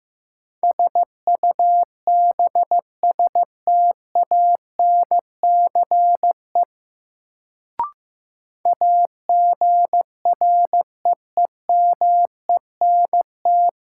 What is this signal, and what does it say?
Morse code, heard as a keyed tone, about 15 wpm, 700 Hz. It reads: SUBSTANCE AGREEMENT
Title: Sentences from Top 1500 Words - New Word Drill Alt - 15wpm